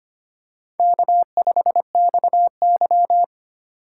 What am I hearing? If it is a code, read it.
Morse code, heard as a keyed tone, about 25 wpm, 700 Hz. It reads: K5XY
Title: USA Call Signs - 4 Characters - 3x Character Spacing - 25wpm